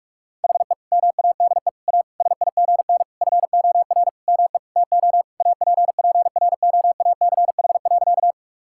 Morse code, heard as a keyed tone, about 42 wpm, 700 Hz. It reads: HE MADE A SIGN FOR ME TO APPROACH.
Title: Sentences from Top 1000 Words - Review Part 6 - 45wpm